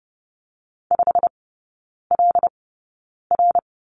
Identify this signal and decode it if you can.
Morse code, heard as a keyed tone, about 30 wpm, spaced as for 9 wpm, 700 Hz. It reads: 5LR
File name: course / 30:10wpm / 087 Lesson 15e - Number 5 - Three Characters 30wpm